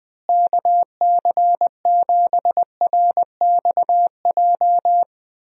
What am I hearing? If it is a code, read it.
Morse code, heard as a keyed tone, about 21 wpm, 700 Hz. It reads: KC7RXJ
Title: USA Call Signs - 6 Characters - Repeated 3x - 20wpm